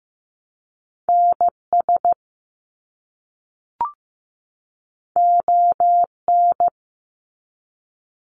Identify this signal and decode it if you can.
Morse code, heard as a keyed tone, about 15 wpm, 700 Hz. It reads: NS ON
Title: Canadian Provinces and Territories 15wpm